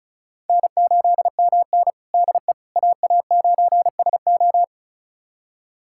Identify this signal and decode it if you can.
Morse code, heard as a keyed tone, about 36 wpm, 700 Hz. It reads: N8MN DE AA9SO
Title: USA Callsign DE Callsign 35wpm